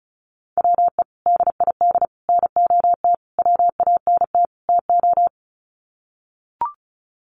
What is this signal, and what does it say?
Morse code, heard as a keyed tone, about 34 wpm, 700 Hz. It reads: WE DID NOT WANT TO
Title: Sets of 5 Words Encore 35wpm